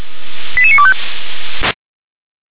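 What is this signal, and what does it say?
Five-tone selective call (ZVEI), heard as quick runs of five tones, tones of about 70 ms. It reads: 88925
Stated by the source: ZVEI five-tone call 88702 received via SDR.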